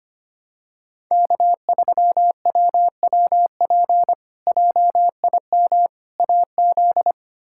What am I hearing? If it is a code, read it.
Morse code, heard as a keyed tone, about 25 wpm, 700 Hz. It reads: K3WWP JIM AZ
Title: K1USN SST QSO Practice - 25wpm